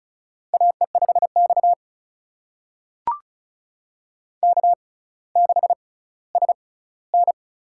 Morse code, heard as a keyed tone, about 36 wpm, spaced as for 18 wpm, 700 Hz. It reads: AE5X K6SN
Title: USA Call Signs - 4 Characters - 6x Character Spacing - 35wpm